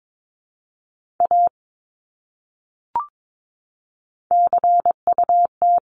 Morse code, heard as a keyed tone, about 22 wpm, 700 Hz. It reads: A CUT
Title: Top 300 Words Repeated 22wpm